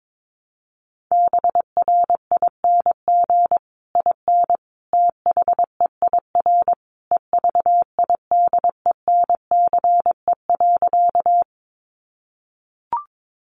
Morse code, heard as a keyed tone, about 22 wpm, 700 Hz. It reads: BRING IN THEIR EVIDENCE.
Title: Sentences from Top 1300 Words 22wpm